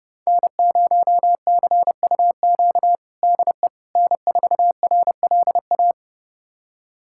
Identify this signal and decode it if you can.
Morse code, heard as a keyed tone, about 30 wpm, 700 Hz. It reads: N0CUQ DE N4RLA
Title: USA Callsign DE Callsign - 5 Characters - 30wpm